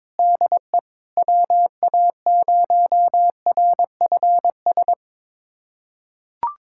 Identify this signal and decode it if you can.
Morse code, heard as a keyed tone, about 22 wpm, 700 Hz. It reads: DE WA0RFS
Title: USA Callsign DE Callsign - 6 Characters - 22wpm